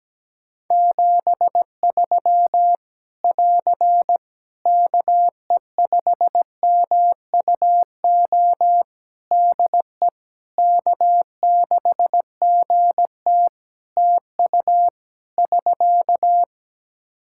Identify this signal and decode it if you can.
Morse code, heard as a keyed tone, about 17 wpm, 700 Hz. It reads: 73 <AR> KE5MUO DE K6GT TU <SK>
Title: Practice QSO - Exchange 4 - Ending - 17wpm